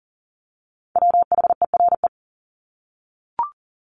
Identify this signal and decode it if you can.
Morse code, heard as a keyed tone, about 40 wpm, 700 Hz. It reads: WHERE
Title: Top 300 Words 40wpm